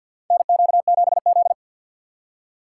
Morse code, heard as a keyed tone, about 52 wpm, 700 Hz. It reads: NQ7Z
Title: USA Call Signs - 4 Characters - 6x Character Spacing - 50wpm